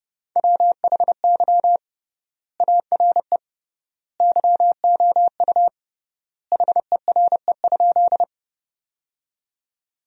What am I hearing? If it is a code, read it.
Morse code, heard as a keyed tone, about 31 wpm, 700 Hz. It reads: WHY ARE YOU HERE?
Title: 3x Word-Spacing - Sentences from Top 200 Words - 30wpm